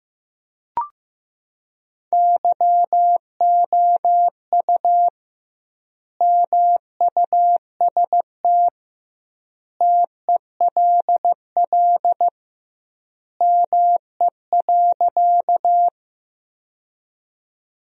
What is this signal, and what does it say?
Morse code, heard as a keyed tone, about 15 wpm, 700 Hz. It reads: YOU MUST TELL ME.
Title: 2x Word-Spacing - Sentences from Top 200 Words - 15wpm